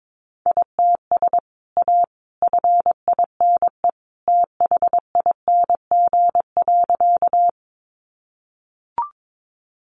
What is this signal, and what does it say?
Morse code, heard as a keyed tone, about 22 wpm, 700 Hz. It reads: ITS A FINE THING.
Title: Sentences from Top 600 Words 22wpm